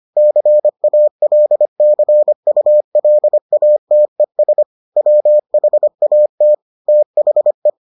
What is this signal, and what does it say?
Morse code, heard as a keyed tone, about 25 wpm, 600 Hz. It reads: CALCULATES WHAT THE